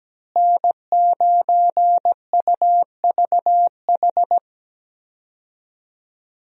Morse code, heard as a keyed tone, about 17 wpm, 700 Hz. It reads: N9UVH
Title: USA Call Signs - 5 Characters - Repeated 3x - 17wpm